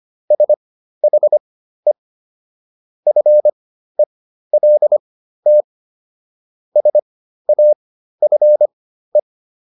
Morse code, heard as a keyed tone, about 25 wpm, spaced as for 13 wpm, 600 Hz. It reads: SHE FELT SAFE